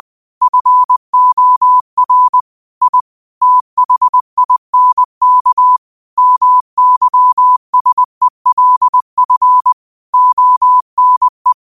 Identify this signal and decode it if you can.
Morse code, heard as a keyed tone, about 20 wpm, 1000 Hz. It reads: FOR I THINK MYSELF ONE
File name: qotdfiles / 20wpm